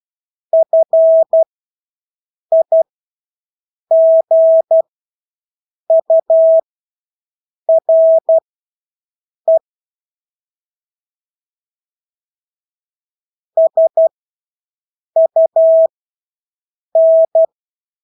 Morse code, heard as a keyed tone, about 12 wpm, spaced as for 6 wpm, 650 Hz. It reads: FIGURE SUN
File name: Engelske ord 12-6 wpm